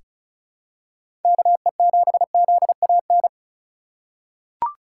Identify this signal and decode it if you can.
Morse code, heard as a keyed tone, about 34 wpm, 700 Hz. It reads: KE7ZAN
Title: USA Call Signs - 6 Characters - 4x Character Spacing - 35wpm